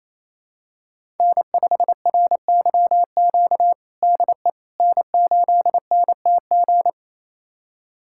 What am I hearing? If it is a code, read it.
Morse code, heard as a keyed tone, about 28 wpm, 700 Hz. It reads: N5RYQ DE N8NTG
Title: USA Callsign DE Callsign - 5 Characters - 28wpm